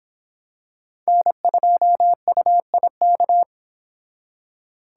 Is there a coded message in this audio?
Yes: Morse code from 1.1 to 3.4 s, heard as a keyed tone, about 25 wpm, 700 Hz. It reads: N2UIK